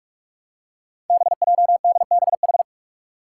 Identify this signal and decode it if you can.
Morse code, heard as a keyed tone, about 46 wpm, 700 Hz. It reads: BJDBH
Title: Five Challenging Letter-Number Combinations 45wpm